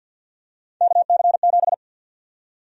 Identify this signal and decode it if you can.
Morse code, heard as a keyed tone, about 51 wpm, 700 Hz. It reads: KC7
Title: USA Call Sign Prefixes - Repeated 2x - 50wpm